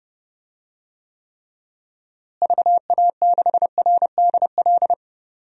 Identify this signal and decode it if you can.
Morse code, heard as a keyed tone, about 30 wpm, 700 Hz. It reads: VA6RDL
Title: DXCC-100 Call Signs - 6 Characters - Repeated 3x - 30wpm